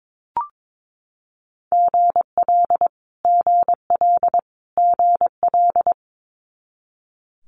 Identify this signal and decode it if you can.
Morse code, heard as a keyed tone, about 22 wpm, 700 Hz. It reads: GL GL GL
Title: Multi-letter Phonemes Repeated 3x 22wpm